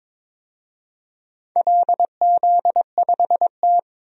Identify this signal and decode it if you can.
Morse code, heard as a keyed tone, about 21 wpm, 700 Hz. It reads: LZ5T
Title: DXCC-100 Call Signs - 4 Characters - Repeated 3x - 22wpm